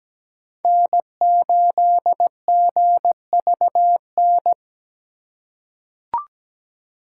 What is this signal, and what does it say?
Morse code, heard as a keyed tone, about 17 wpm, 700 Hz. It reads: N8GVN
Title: USA Call Signs - 5 Characters - 2x Character Spacing - 17wpm